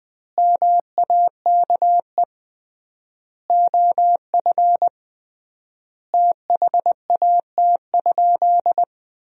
Morse code, heard as a keyed tone, about 21 wpm, 700 Hz. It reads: MAKE OF THAT?
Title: 3x Word-Spacing - Sentences from Top 100 Words - 20wpm